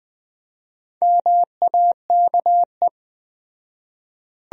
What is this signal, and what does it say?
Morse code, heard as a keyed tone, about 20 wpm, 700 Hz. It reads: MAKE